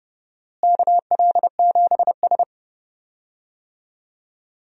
Morse code, heard as a keyed tone, about 31 wpm, 700 Hz. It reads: KL7S
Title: DXCC-100 Call Signs - 4 Characters - Repeated 2x - 30wpm